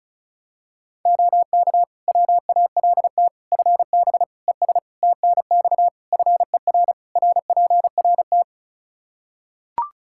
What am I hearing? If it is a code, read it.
Morse code, heard as a keyed tone, about 34 wpm, 700 Hz. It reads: OK WALT FB ES TNX FER RPRT
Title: Practice QSO - Exchange 2 - Info - 35wpm